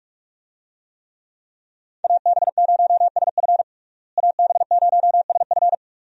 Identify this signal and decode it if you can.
Morse code, heard as a keyed tone, about 42 wpm, 700 Hz. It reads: AB0SF AB0SF
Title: USA Call Signs - 5 Characters - Repeated 3x - 45wpm